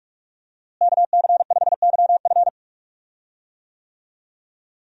Morse code, heard as a keyed tone, about 46 wpm, 700 Hz. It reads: KC5YF
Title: USA Call Signs - 5 Characters - Repeated 2x - 45wpm